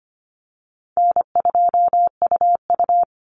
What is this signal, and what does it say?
Morse code, heard as a keyed tone, about 25 wpm, 700 Hz. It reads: N2UU
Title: Easy Call Signs Repeated 2x - 25wpm